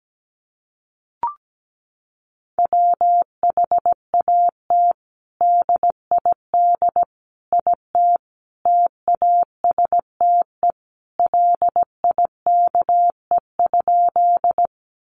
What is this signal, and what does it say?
Morse code, heard as a keyed tone, about 17 wpm, 700 Hz. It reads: WHAT DID IT TASTE LIKE?
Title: Sentences from Top 1100 Words 17wpm